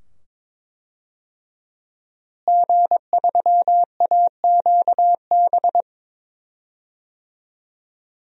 Morse code, heard as a keyed tone, about 22 wpm, 700 Hz. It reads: G3AQB
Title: DXCC-100 Call Signs - 5 Characters - Repeated 2x - 22wpm